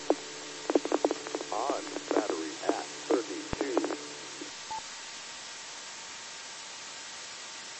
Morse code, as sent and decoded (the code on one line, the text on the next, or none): none